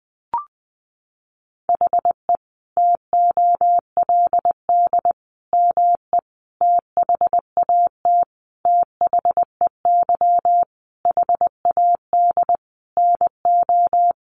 Morse code, heard as a keyed tone, about 20 wpm, 700 Hz. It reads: HE TOLD ME THAT THEY HAD NO
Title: Sentences from Top 1300 Words 20wpm